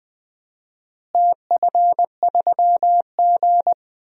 Morse code, heard as a keyed tone, about 20 wpm, 700 Hz. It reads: TF3G
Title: DXCC-100 Call Signs - 4 Characters - Repeated 2x - 20wpm